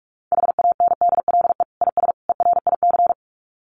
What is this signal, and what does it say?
Morse code, heard as a keyed tone, about 44 wpm, 700 Hz. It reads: HANDLE IS ERIC
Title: Common QSO Phrases 45wpm